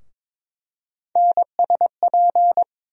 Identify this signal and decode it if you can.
Morse code, heard as a keyed tone, about 22 wpm, 700 Hz. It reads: NSP
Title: USA Call Sign Postfixes - 3 Letters 22wpm